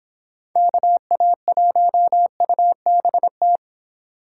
Morse code, heard as a keyed tone, about 26 wpm, 700 Hz. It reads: KA1UBT